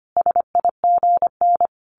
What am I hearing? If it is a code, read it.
Morse code, heard as a keyed tone, about 25 wpm, 700 Hz. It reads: SIGN
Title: A Sound Of Thunder 25wpm